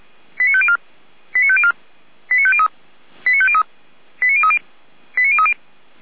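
Five-tone selective call (ZVEI), heard as quick runs of five tones, tones of about 70 ms. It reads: 89504 89504 89503 89503 89030 89030
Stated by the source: ZVEI five-tone selective call received on an SDR.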